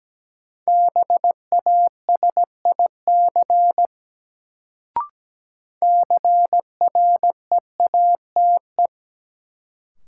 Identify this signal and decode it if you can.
Morse code, heard as a keyed tone, about 17 wpm, 700 Hz. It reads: BASIC CREATE
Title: Sentences from Top 1000 Words - Review Part 3 - New Word Drill - 17wpm